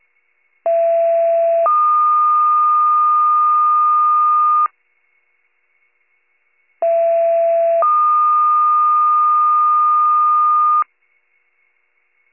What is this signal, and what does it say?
Two-tone page, heard as a one-second tone, then a three-second tone, about 670 then 1220 Hz, about 670 then 1220 Hz. It is heard from 0.7 to 10.8 s.